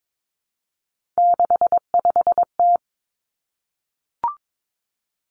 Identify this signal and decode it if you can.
Morse code, heard as a keyed tone, about 22 wpm, 700 Hz. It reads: 65T
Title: Three Challenging Letter-Number Combinations 22wpm